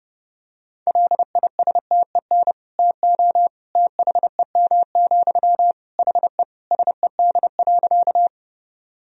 Morse code, heard as a keyed tone, about 31 wpm, 700 Hz. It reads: LISTEN TO THEM, HE SED.
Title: Sentences from Top 400 Words and Abbreviations 30wpm